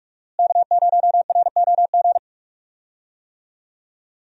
Morse code, heard as a keyed tone, about 46 wpm, 700 Hz. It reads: K0ROG
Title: USA Call Signs - 5 Characters - Repeated 3x - 45wpm